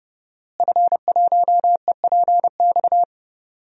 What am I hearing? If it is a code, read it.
Morse code, heard as a keyed tone, about 32 wpm, 700 Hz. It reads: F1EPX
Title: DXCC-100 Call Signs - 5 Characters - Repeated 2x - 30wpm